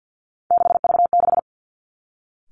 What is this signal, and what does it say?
Morse code, heard as a keyed tone, about 50 wpm, 700 Hz. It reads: 6V6
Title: Three Challenging Letter-Number Combinations 50wpm